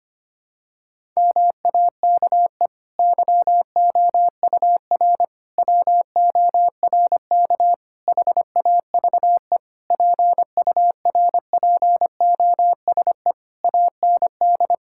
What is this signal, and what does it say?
Morse code, heard as a keyed tone, about 25 wpm, 700 Hz. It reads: MAKE YOUR WORK HAVE PURPOSE AND
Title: Sentences from Top 1000 Words - Review Part 6 - 25wpm